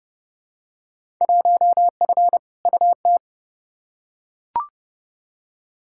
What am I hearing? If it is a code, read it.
Morse code, heard as a keyed tone, about 30 wpm, 700 Hz. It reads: 1F UT
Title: Field Day Exchanges Encore - 30wpm